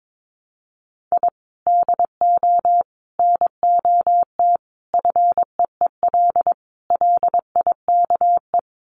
Morse code, heard as a keyed tone, about 22 wpm, 700 Hz. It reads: I DO NOT FEEL LIKE
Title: Sets of 5 Words Encore 22wpm